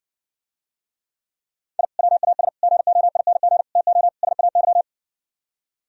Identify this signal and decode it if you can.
Morse code, heard as a keyed tone, about 60 wpm, 700 Hz. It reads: I WAS GOING TO SAY